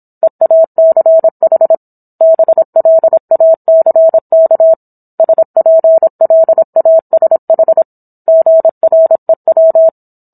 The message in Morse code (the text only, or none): EACH BLACK SPLASH GREW